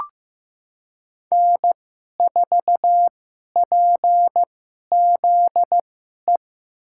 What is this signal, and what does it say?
Morse code, heard as a keyed tone, about 15 wpm, spaced as for 11 wpm, 700 Hz. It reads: N4PZE
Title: USA Call Signs - 5 Characters - 2x Character Spacing - 15wpm